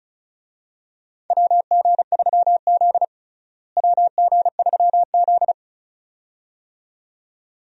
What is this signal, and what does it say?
Morse code, heard as a keyed tone, about 34 wpm, 700 Hz. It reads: WG3Z WG3Z
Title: USA Call Signs - 4 Characters - Repeated 2x - 35wpm